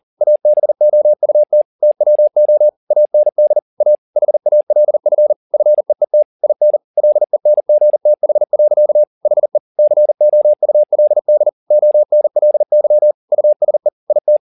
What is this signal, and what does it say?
Morse code, heard as a keyed tone, about 40 wpm, 600 Hz. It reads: ABOUT TWO AND A HALF FEET IN LENGTH. HE COULD ONLY USE IT
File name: the old man and the sea 2 40wpm